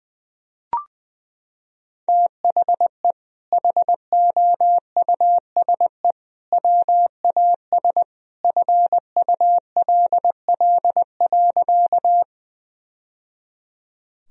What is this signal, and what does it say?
Morse code, heard as a keyed tone, about 20 wpm, 700 Hz. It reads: THE HOUSE WAS FULL.
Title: Sentences from Top 1000 Words - Review Part 3 - 20wpm